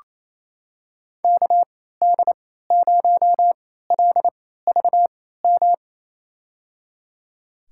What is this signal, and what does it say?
Morse code, heard as a keyed tone, about 28 wpm, spaced as for 16 wpm, 700 Hz. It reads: KD0LVM
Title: USA Call Signs - 6 Characters - 3x Character Spacing - 28wpm